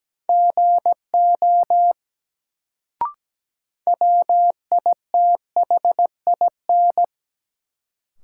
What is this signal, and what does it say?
Morse code, heard as a keyed tone, about 17 wpm, 700 Hz. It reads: GO WITHIN
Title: Sets of 2 Words - New Word Drill - 17wpm